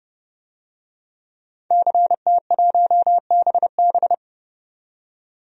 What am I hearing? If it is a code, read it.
Morse code, heard as a keyed tone, about 30 wpm, 700 Hz. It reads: CT1BB
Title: DXCC-100 Call Signs - 5 Characters - 30wpm